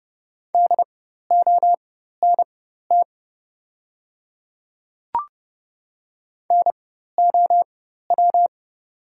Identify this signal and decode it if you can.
Morse code, heard as a keyed tone, about 30 wpm, spaced as for 14 wpm, 700 Hz. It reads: DONT NOW
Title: 4x Character Spacing - Top 300 Words - 30wpm